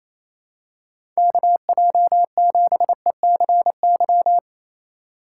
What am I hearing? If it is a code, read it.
Morse code, heard as a keyed tone, about 28 wpm, 700 Hz. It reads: KJ7ECY
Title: USA Call Signs - 6 Characters - 2x Character Spacing - 28wpm